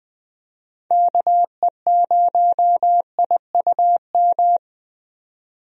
Morse code, heard as a keyed tone, about 20 wpm, 700 Hz. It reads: KE0IUM
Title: USA Call Signs - 6 Characters - 20wpm